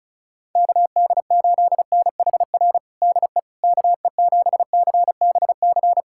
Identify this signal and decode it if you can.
Morse code, heard as a keyed tone, about 34 wpm, 700 Hz. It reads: KD8NHR DE KE7CBC
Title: USA Callsign DE Callsign 35wpm